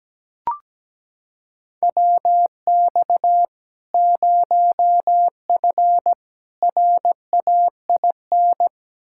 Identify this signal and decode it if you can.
Morse code, heard as a keyed tone, about 17 wpm, 700 Hz. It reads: WX 0F RAIN
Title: Practice QSO - Exchange 2 - Info - 17wpm